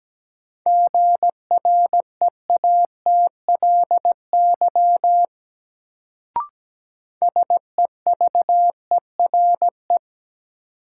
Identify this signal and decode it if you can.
Morse code, heard as a keyed tone, about 17 wpm, 700 Hz. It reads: GREATLY SEVERE
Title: Sentences from Top 1400 Words - New Word Drill - 17wpm